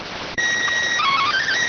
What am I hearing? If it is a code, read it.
SSTV (slow-scan television), PD 290.